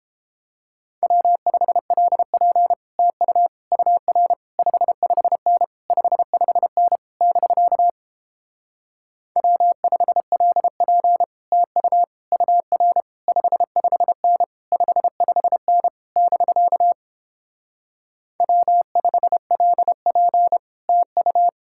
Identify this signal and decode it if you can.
Morse code, heard as a keyed tone, about 29 wpm, 700 Hz. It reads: W5LP TU UR 55N 55N <BK> W5LP TU UR 55N 55N <BK> W5LP TU